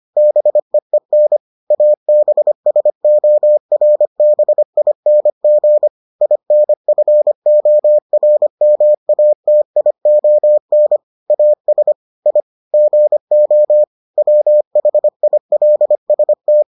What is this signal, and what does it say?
Morse code, heard as a keyed tone, about 25 wpm, 600 Hz. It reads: BEEN ABSORBING INFORMATION AS I GO WHILST